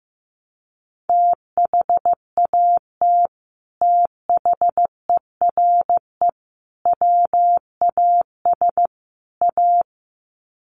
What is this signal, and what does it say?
Morse code, heard as a keyed tone, about 15 wpm, 700 Hz. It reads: THAT THERE WAS A
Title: Sets of 4 Words 15wpm